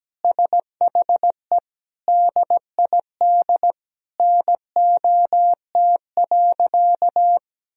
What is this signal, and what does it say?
Morse code, heard as a keyed tone, about 17 wpm, 700 Hz. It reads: SHE DID NOT.
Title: Sentences from Top 1000 Words - Review Part 1 - 17wpm